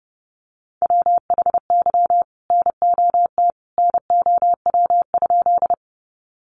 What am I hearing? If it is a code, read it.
Morse code, heard as a keyed tone, about 30 wpm, 700 Hz. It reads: WHY NOT NOW?
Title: Sentences from Top 600 Words 30wpm